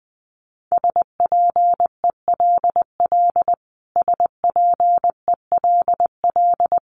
Morse code, heard as a keyed tone, about 20 wpm, 700 Hz. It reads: SPELL SPELL
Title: Top 200 Words Repeated 20wpm